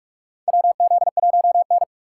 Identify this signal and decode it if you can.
Morse code, heard as a keyed tone, about 46 wpm, 700 Hz. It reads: WZ1N
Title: USA Call Signs - 4x Character Spacing - 45wpm